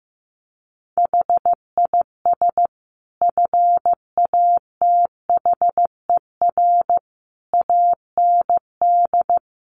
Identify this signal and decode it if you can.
Morse code, heard as a keyed tone, about 15 wpm, 700 Hz. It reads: HIS FATHER AND
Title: Sentences from Top 1500 Words 15wpm